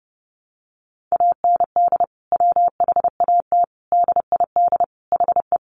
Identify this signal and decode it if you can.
Morse code, heard as a keyed tone, about 30 wpm, 700 Hz. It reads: AND WHAT DID HE
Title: Sentences from Top 200 Words 30wpm